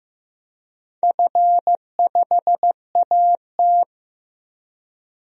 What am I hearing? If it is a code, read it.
Morse code, heard as a keyed tone, about 15 wpm, 700 Hz. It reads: F5AT
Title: DXCC-100 Call Signs - 4 Characters - Repeated 2x - 15wpm